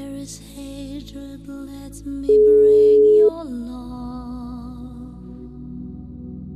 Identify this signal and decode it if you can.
Ringback tone (Europe, 425 Hz).